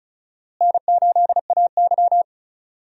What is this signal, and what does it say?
Morse code, heard as a keyed tone, about 34 wpm, 700 Hz. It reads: N8AY
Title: USA Call Signs - 4 Characters - Repeated 3x - 35wpm